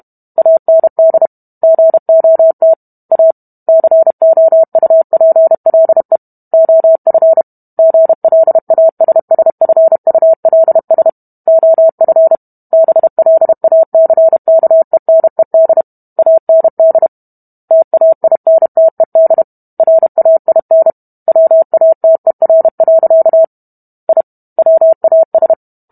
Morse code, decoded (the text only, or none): AND GOT A COUPLE OF GLASSFULS OF BLACKENED AND TAINTED RAIN WATER. I WAS